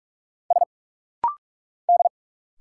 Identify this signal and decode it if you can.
Morse code, heard as a keyed tone, about 46 wpm, 700 Hz. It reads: S D
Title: Single Letters - Rapid-Fire 45wpm